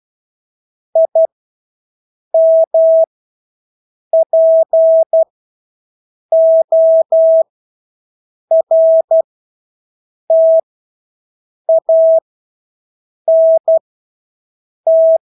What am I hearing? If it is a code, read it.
Morse code, heard as a keyed tone, about 12 wpm, spaced as for 6 wpm, 650 Hz. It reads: IMPORTANT
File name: Engelske ord 12-6 wpm